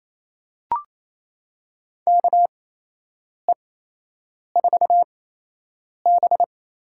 Morse code, heard as a keyed tone, about 27 wpm, spaced as for 8 wpm, 700 Hz. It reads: KE4B
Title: USA Call Signs - 6 Characters - 8x Character Spacing - 28wpm